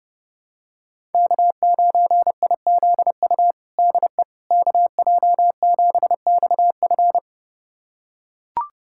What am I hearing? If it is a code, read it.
Morse code, heard as a keyed tone, about 30 wpm, 700 Hz. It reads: K9IZU DE KJ7XF
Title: USA Callsign DE Callsign - 5 Characters - 30wpm